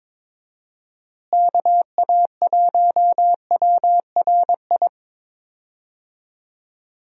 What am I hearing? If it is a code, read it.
Morse code, heard as a keyed tone, about 22 wpm, 700 Hz. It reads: KA1WRI
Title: USA Call Signs - 6 Characters - Repeated 2x - 22wpm